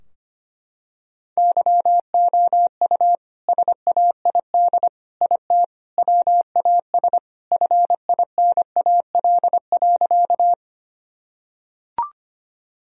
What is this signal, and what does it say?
Morse code, heard as a keyed tone, about 25 wpm, 700 Hz. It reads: YOU SAID IT WAS FINAL.
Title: Sentences from Top 1000 Words - Review Part 2 - 25wpm